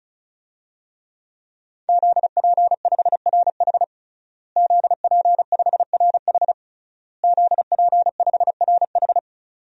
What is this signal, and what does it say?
Morse code, heard as a keyed tone, about 36 wpm, 700 Hz. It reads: ZP5RH ZP5RH ZP5RH
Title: DXCC-100 Call Signs - 5 Characters - Repeated 3x - 35wpm